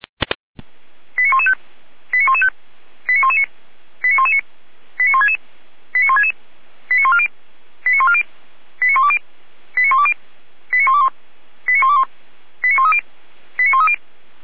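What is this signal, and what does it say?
Five-tone selective call (ZVEI), heard as quick runs of five tones, tones of about 70 ms. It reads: 89115 89115 89119 89119 89166 89166 89140 89140 89120 89120 89121 89121 89130 89130
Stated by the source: ZVEI five-tone selective call 89110 received with an SDR.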